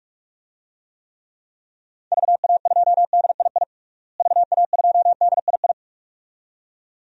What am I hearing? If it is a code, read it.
Morse code, heard as a keyed tone, about 43 wpm, 700 Hz. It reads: VA2DII VA2DII
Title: DXCC-100 Call Signs - 6 Characters - Repeated 2x - 45wpm